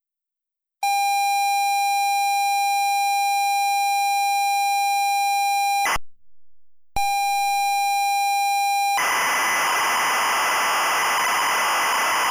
ZX Spectrum tape loading.
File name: Forums / SpectrumComputing-BlackMirror-Bandersnatch-ZXSpectrumTape